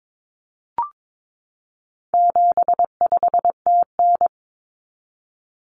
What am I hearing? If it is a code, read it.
Morse code, heard as a keyed tone, about 22 wpm, 700 Hz. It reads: 75TN
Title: Common Cut Numbers - 4-Digit 22wpm